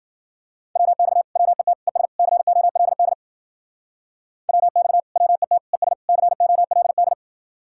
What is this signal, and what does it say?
Morse code, heard as a keyed tone, about 52 wpm, 700 Hz. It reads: WX WET ES COLD WX WET ES COLD